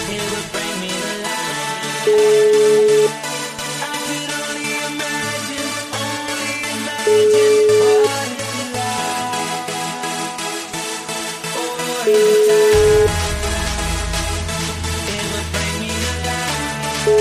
Ringback tone (Europe, 425 Hz), with noise.